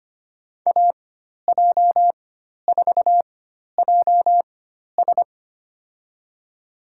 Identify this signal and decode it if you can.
Morse code, heard as a keyed tone, about 25 wpm, spaced as for 12 wpm, 700 Hz. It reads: AJ4JS
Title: USA Call Signs - 5 Characters - 4x Character Spacing - 25wpm